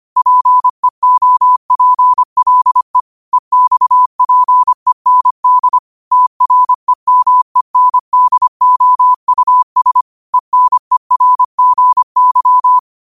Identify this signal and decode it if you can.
Morse code, heard as a keyed tone, about 25 wpm, 1000 Hz. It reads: PEOPLE EXPEND TREMENDOUS ENERGY
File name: qotdfiles / 25wpm